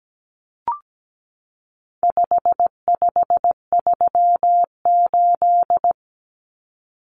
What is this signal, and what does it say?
Morse code, heard as a keyed tone, about 17 wpm, 700 Hz. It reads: 5538
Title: Common Cut Numbers - 4-Digit 17wpm